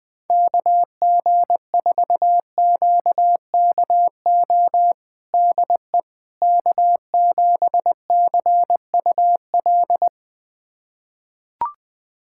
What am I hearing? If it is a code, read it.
Morse code, heard as a keyed tone, about 21 wpm, 690 Hz. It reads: KG4QKO DE K7CUL
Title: USA Callsign DE Callsign 20wpm